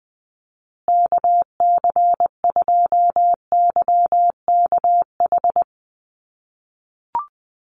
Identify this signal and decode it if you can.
Morse code, heard as a keyed tone, about 20 wpm, 700 Hz. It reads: KC2YKH
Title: USA Call Signs Repeated 2x 20wpm